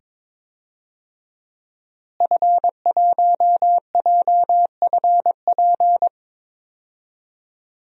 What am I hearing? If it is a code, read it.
Morse code, heard as a keyed tone, about 22 wpm, 700 Hz. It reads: F1JFP
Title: DXCC-100 Call Signs - 5 Characters - Repeated 2x - 22wpm